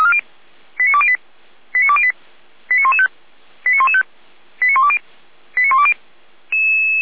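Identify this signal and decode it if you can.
Five-tone selective call (ZVEI), heard as quick runs of five tones, tones of about 80 ms. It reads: 89208 89208 89115 89115 89120 89120
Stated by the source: ZVEI five-tone selective call received by SDR.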